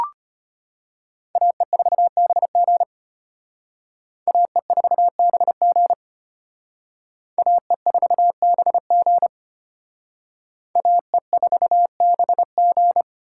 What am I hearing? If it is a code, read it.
Morse code, heard as a keyed tone, about 33 wpm, 700 Hz. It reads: AE4BG AE4BG AE4BG AE4BG